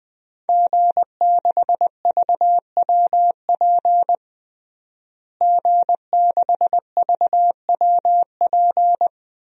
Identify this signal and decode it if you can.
Morse code, heard as a keyed tone, about 20 wpm, 700 Hz. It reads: G6VWP G6VWP
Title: DXCC-100 Call Signs - 5 Characters - Repeated 2x - 20wpm